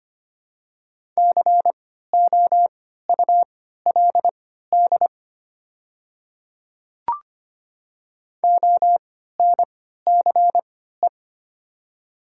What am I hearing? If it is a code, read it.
Morse code, heard as a keyed tone, about 25 wpm, spaced as for 14 wpm, 690 Hz. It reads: COULD ONCE
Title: 3x Character Spacing - Top 300 Words - 25wpm